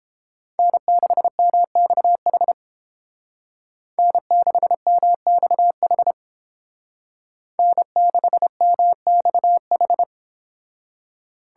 Morse code, heard as a keyed tone, about 29 wpm, 700 Hz. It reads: N6MXH N6MXH N6MXH